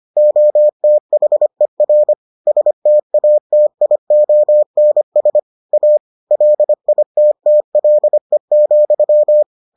Morse code, heard as a keyed tone, about 25 wpm, 600 Hz. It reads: OTHER STATIONS A LITTLE,